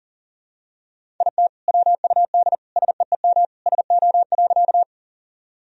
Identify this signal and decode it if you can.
Morse code, heard as a keyed tone, about 40 wpm, 700 Hz. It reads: IT WUD SEEM SO.
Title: Sentences from Top 300 Words and Abbreviations 40wpm